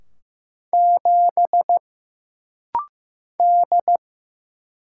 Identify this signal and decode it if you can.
Morse code, heard as a keyed tone, about 15 wpm, 700 Hz. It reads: 7 D
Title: Single Letter-Number Alt - Rapid-Fire 15wpm